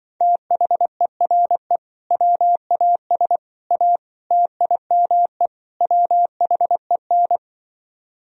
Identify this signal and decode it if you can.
Morse code, heard as a keyed tone, about 24 wpm, 700 Hz. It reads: THERE WAS A TIME WHEN
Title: Speed-Racing - Sets of 5 Words 20wpm